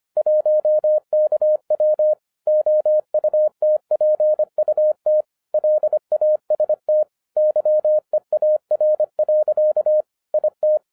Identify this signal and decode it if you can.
Morse code, heard as a keyed tone, about 25 wpm, 610 Hz. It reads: 1KW OUTPUT LAST YEAR. IT
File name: 2011-10-mp3 / n04-JUST MOVED 25wpm